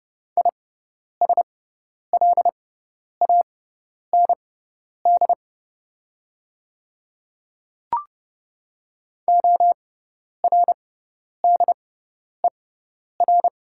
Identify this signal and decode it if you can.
Morse code, heard as a keyed tone, about 31 wpm, spaced as for 10 wpm, 700 Hz. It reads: ISLAND ORDER
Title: 6x Character Spacing - Top 500 Words - 30wpm